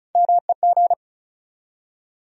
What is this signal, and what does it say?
Morse code, heard as a keyed tone, about 34 wpm, 700 Hz. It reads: MEG
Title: USA Common Names - Max 5 Letters - 35wpm